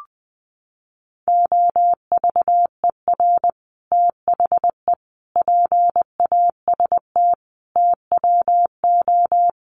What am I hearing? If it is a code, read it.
Morse code, heard as a keyed tone, about 20 wpm, 700 Hz. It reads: OVER THE PAST TWO
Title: Sets of 5 Words Encore2 20wpm